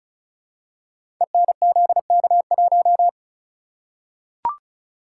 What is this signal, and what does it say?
Morse code, heard as a keyed tone, about 36 wpm, 700 Hz. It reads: ENZK1
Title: Code Groups - 5 Letter:Numbers - 35wpm